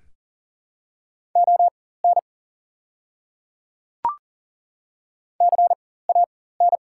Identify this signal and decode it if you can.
Morse code, heard as a keyed tone, about 40 wpm, spaced as for 19 wpm, 700 Hz. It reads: ON CAN
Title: 4x Character Spacing - Top 100 Words - 40wpm